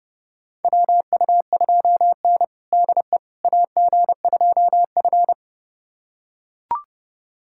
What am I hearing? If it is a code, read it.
Morse code, heard as a keyed tone, about 31 wpm, 700 Hz. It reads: WU2N DE AG2F
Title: USA Callsign DE Callsign - 4 Characters - 30wpm